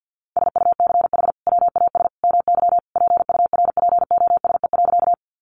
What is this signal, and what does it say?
Morse code, heard as a keyed tone, about 50 wpm, 700 Hz. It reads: SUCH WAS MY PURPOSE.
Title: Sentences from Top 800 Words 50wpm